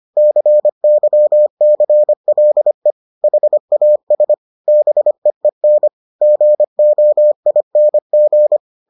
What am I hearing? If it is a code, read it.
Morse code, heard as a keyed tone, about 25 wpm, 600 Hz. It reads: CYCLE HAS BEEN GOING